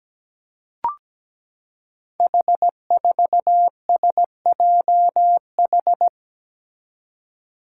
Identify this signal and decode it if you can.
Morse code, heard as a keyed tone, about 17 wpm, 700 Hz. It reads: H4SJH
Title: Five Challenging Letter-Number Combinations 17wpm